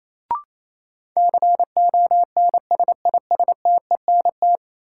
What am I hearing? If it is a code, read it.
Morse code, heard as a keyed tone, about 28 wpm, 700 Hz. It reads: CONSISTENT
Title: Sentences from Top 1500 Words - New Word Drill Alt - 28wpm